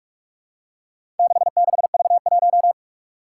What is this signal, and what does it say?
Morse code, heard as a keyed tone, about 41 wpm, 700 Hz. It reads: 66V1
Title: Four Challenging Letter-Number Combinations 45wpm